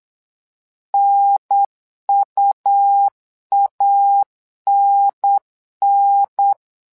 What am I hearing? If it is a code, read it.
Morse code, heard as a keyed tone, about 9 wpm, 800 Hz. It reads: NUANN
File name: LEZIONI / 2 UHDN / 2 UHDN 8wpm2232